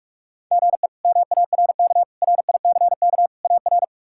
Morse code, heard as a keyed tone, about 42 wpm, 700 Hz. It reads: GE MARK RICK AR
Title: K1USN SST QSO Practice - 45wpm